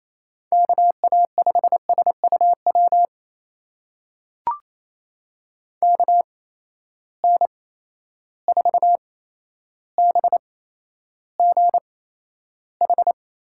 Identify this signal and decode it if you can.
Morse code, heard as a keyed tone, about 28 wpm, spaced as for 12 wpm, 700 Hz. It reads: KA5SUW KN4BGH
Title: USA Call Signs - 6 Characters - 8x Character Spacing - 28wpm